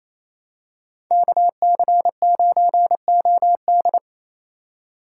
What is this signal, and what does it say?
Morse code, heard as a keyed tone, about 28 wpm, 700 Hz. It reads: KC9OD
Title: USA Call Signs - 4x Character Spacing - 28wpm